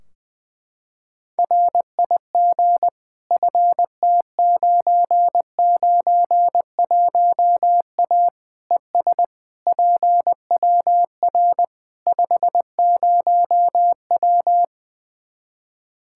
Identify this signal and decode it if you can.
Morse code, heard as a keyed tone, about 20 wpm, 700 Hz. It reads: RIG FT991A ES PWR 50W
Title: K1USN SST QSO Practice 20wpm